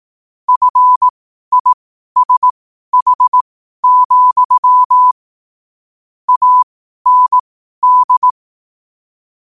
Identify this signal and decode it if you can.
Morse code, heard as a keyed tone, about 18 wpm, spaced as for 12 wpm, 1000 Hz. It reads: FISH, AND
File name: qotdfiles / 12wpm